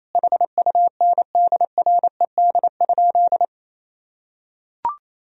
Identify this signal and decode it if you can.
Morse code, heard as a keyed tone, about 27 wpm, 700 Hz. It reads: HUNDRED?
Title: Sentences from Top 1000 Words - Review Part 2 - 28wpm